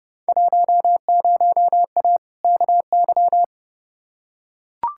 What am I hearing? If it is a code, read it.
Morse code, heard as a keyed tone, about 30 wpm, 700 Hz. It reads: 10A KY
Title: Field Day Exchanges Encore - 30wpm